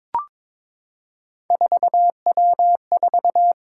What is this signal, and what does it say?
Morse code, heard as a keyed tone, about 23 wpm, 700 Hz. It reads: 4W4
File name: practice / 22wpm / Three Letter-Number Combinations - 22wpm